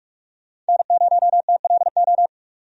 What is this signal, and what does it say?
Morse code, heard as a keyed tone, about 46 wpm, 700 Hz. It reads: N0TLO
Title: USA Call Signs - 5 Characters - 45wpm